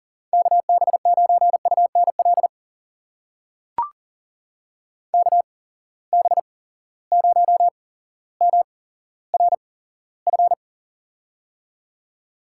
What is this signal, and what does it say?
Morse code, heard as a keyed tone, about 39 wpm, spaced as for 17 wpm, 700 Hz. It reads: KB9UNL KB0MRF
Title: USA Call Signs - 6 Characters - 8x Character Spacing - 40wpm